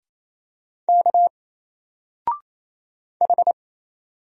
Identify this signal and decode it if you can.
Morse code, heard as a keyed tone, about 27 wpm, 700 Hz. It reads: K H
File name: Single-Letter-Number-28wpm